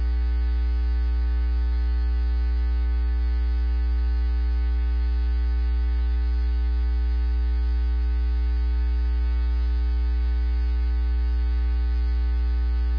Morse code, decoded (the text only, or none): none